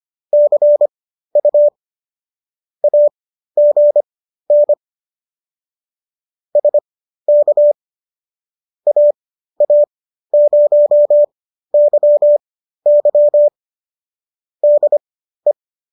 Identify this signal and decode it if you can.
Morse code, heard as a keyed tone, about 25 wpm, spaced as for 13 wpm, 590 Hz. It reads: CU AGN SK AA0YY DE